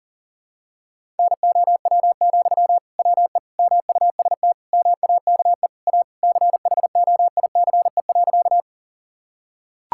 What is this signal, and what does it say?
Morse code, heard as a keyed tone, about 40 wpm, 700 Hz. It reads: NOW, WE MUST MAKE A CHOICE.
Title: Sentences from Top 1000 Words - Review Part 6 - 40wpm